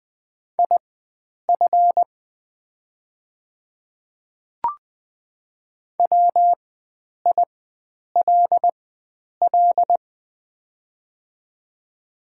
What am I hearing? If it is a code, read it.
Morse code, heard as a keyed tone, about 20 wpm, spaced as for 9 wpm, 690 Hz. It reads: IF WILL
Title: 4x Character Spacing - Top 100 Words - 20wpm